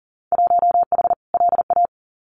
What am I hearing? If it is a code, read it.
Morse code, heard as a keyed tone, about 40 wpm, 700 Hz. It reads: 1H LA
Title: Winter Field Day Exchanges 40wpm